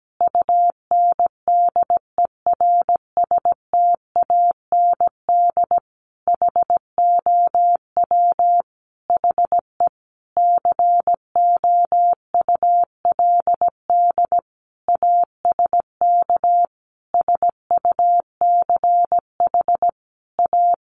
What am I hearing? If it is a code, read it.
Morse code, heard as a keyed tone, about 17 wpm, 700 Hz. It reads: UNDERSTAND HOW HE COULD ASK SUCH A
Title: Sentences from Top 600 Words 17wpm